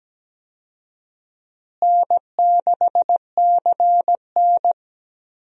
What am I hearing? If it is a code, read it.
Morse code, heard as a keyed tone, about 17 wpm, 700 Hz. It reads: N6CN
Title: USA Call Signs - 4 Characters - Repeated 3x - 17wpm